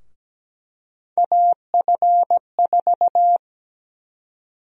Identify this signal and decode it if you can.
Morse code, heard as a keyed tone, about 17 wpm, 700 Hz. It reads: AF4
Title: USA Call Sign Prefixes - 17wpm